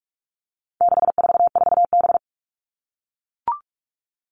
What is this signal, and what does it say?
Morse code, heard as a keyed tone, about 44 wpm, 700 Hz. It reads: 644B
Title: All Cut Numbers - 4-Digit 45wpm